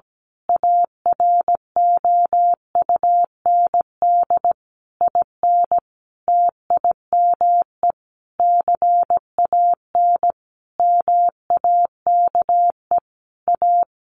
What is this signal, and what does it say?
Morse code, heard as a keyed tone, about 17 wpm, 700 Hz. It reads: AROUND IN TIME CAN MAKE A
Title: A Sound Of Thunder 17wpm